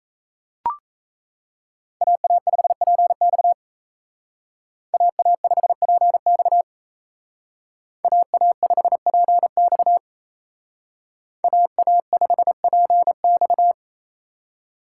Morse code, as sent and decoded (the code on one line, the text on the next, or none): .- .- ..... .--. -..- / .- .- ..... .--. -..- / .- .- ..... .--. -..- / .- .- ..... .--. -..-
AA5PX AA5PX AA5PX AA5PX